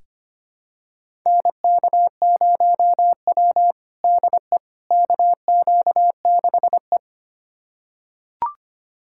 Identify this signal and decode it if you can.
Morse code, heard as a keyed tone, about 25 wpm, 700 Hz. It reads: NK0W DE KQ6E
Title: USA Callsign DE Callsign - 4 Characters - 25wpm